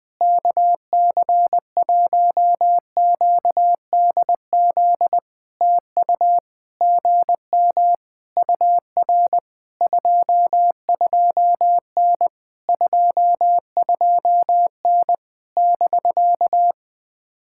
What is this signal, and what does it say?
Morse code, heard as a keyed tone, about 20 wpm, 700 Hz. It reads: KC1QDZ TU GM UR 22N 22N <BK>
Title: POTA QSO Practice - 20wpm